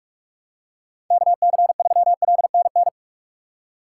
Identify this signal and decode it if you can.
Morse code, heard as a keyed tone, about 46 wpm, 700 Hz. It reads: KC3LNN
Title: USA Call Signs - 6 Characters - 3x Character Spacing - 45wpm